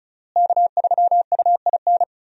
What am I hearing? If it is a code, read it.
Morse code, heard as a keyed tone, about 36 wpm, 700 Hz. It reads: K3UIN
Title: USA Call Signs - 5 Characters - Repeated 3x - 35wpm